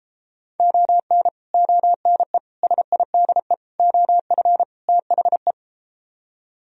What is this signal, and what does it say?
Morse code, heard as a keyed tone, about 33 wpm, 700 Hz. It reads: ON ONE SIDE OF THE